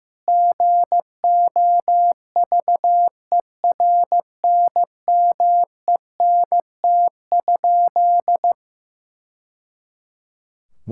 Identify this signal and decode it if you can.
Morse code, heard as a keyed tone, about 15 wpm, 700 Hz. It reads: GOVERNMENT?
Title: Sentences from Top 1000 Words - Review Part 5 - 15wpm